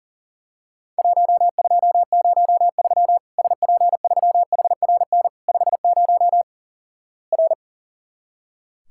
Morse code, heard as a keyed tone, about 43 wpm, 700 Hz. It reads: 1203 SP3HRN 50 R
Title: Worked All Europe DX Contest - QTC Exchange - 40wpm